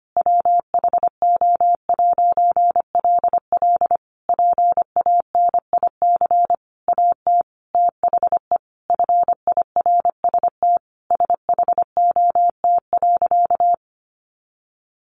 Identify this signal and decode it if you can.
Morse code, heard as a keyed tone, about 25 wpm, 700 Hz. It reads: WHO'LL PANIC AT THE FIRST SHOT.
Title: A Sound Of Thunder 25wpm